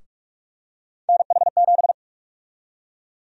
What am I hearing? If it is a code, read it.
Morse code, heard as a keyed tone, about 43 wpm, 700 Hz. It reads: NH7
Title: USA Call Sign Prefixes - Repeated 2x - 45wpm